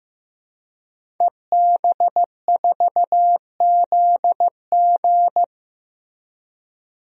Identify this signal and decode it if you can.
Morse code, heard as a keyed tone, about 15 wpm, 700 Hz. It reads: EB4ZG
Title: DXCC-100 Call Signs - 5 Characters - Repeated 3x - 15wpm